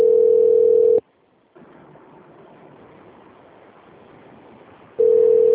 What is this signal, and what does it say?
Ringback tone (North America), 440 + 480 Hz.